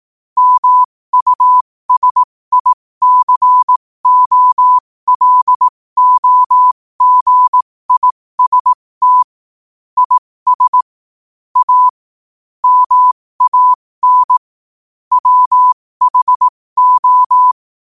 Morse code, heard as a keyed tone, about 18 wpm, spaced as for 15 wpm, 1000 Hz. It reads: MUSICOLOGIST IS A MAN WHO
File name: qotdfiles / 15wpm